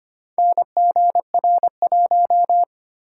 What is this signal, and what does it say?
Morse code, heard as a keyed tone, about 25 wpm, 690 Hz. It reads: NGR1
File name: practice / 25wpm / Four Letter-Number Combinations - 25wpm